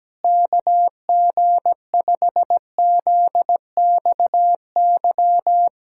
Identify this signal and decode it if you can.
Morse code, heard as a keyed tone, about 17 wpm, 700 Hz. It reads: KG5ZXY
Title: USA Callsign DE Callsign 17wpm